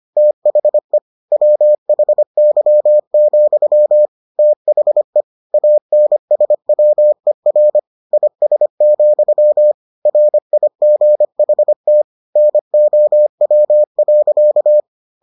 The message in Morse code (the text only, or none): THE WHY, THE ANSWER IS, RIGHT NOW.